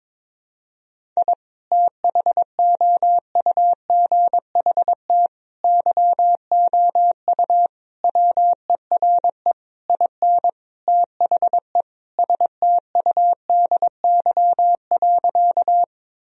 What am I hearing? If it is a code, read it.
Morse code, heard as a keyed tone, about 22 wpm, 700 Hz. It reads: I THOUGHT YOU WERE IN THE STUDY.
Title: Sentences from Top 1000 Words - Review Part 1 - 22wpm